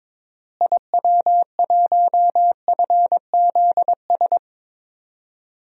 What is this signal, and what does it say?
Morse code, heard as a keyed tone, about 22 wpm, 700 Hz. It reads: IW1FZS
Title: DXCC-100 Call Signs - 6 Characters - Repeated 2x - 22wpm